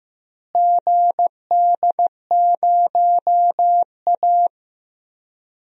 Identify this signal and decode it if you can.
Morse code, heard as a keyed tone, about 15 wpm, 700 Hz. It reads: GD0A
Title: DXCC-100 Call Signs - 4 Characters - Repeated 3x - 15wpm